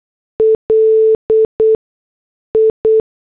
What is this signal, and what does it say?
Morse code, heard as a keyed tone, about 8 wpm, spaced as for 6 wpm, 440 Hz. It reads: LI